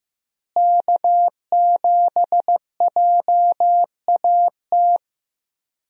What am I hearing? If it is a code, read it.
Morse code, heard as a keyed tone, about 15 wpm, 700 Hz. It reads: K7JAT
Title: USA Call Signs - 5 Characters - Repeated 2x - 15wpm